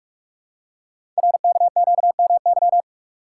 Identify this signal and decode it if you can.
Morse code, heard as a keyed tone, about 44 wpm, 700 Hz. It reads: RKQMY
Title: Code Groups - 5 Letter:Numbers - 45wpm